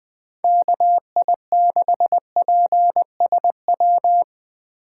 Morse code, heard as a keyed tone, about 20 wpm, 700 Hz. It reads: KI6PSW
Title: USA Call Signs - 6 Characters - 2x Character Spacing - 20wpm